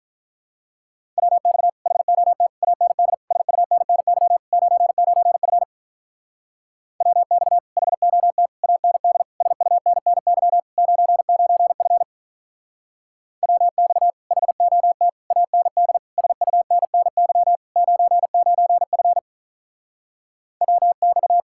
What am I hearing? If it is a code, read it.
Morse code, heard as a keyed tone, about 44 wpm, 700 Hz. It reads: WX HOT AND SUNNY 99F WX HOT AND SUNNY 99F WX HOT AND SUNNY 99F WX
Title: Speed-Racing Common QSO Phrases Encore - 35wpm